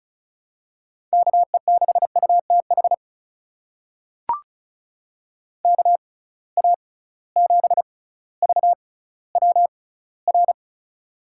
Morse code, heard as a keyed tone, about 36 wpm, spaced as for 18 wpm, 700 Hz. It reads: KE6UTH KA7VWR
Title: USA Call Signs - 6 Characters - 6x Character Spacing - 35wpm